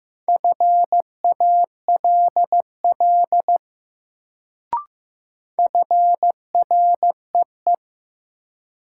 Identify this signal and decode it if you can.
Morse code, heard as a keyed tone, about 15 wpm, 700 Hz. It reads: FALL FREE
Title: Sentences from Top 500 Words - New Word Drill - 15wpm